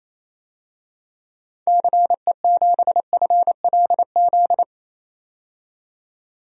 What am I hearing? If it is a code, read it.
Morse code, heard as a keyed tone, about 28 wpm, 700 Hz. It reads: CE7FLZ
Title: DXCC-100 Call Signs - 6 Characters - 28wpm